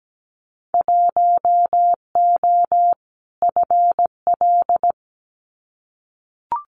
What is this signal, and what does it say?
Morse code, heard as a keyed tone, about 17 wpm, 700 Hz. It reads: 1O FL
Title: Winter Field Day Exchanges 17wpm